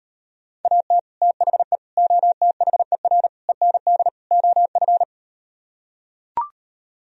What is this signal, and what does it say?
Morse code, heard as a keyed tone, about 38 wpm, 700 Hz. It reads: AT THE OTHER END OF